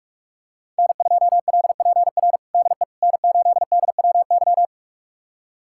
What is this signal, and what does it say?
Morse code, heard as a keyed tone, about 41 wpm, 700 Hz. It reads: N2LPR DE N8DWY
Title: USA Callsign DE Callsign - 5 Characters - 45wpm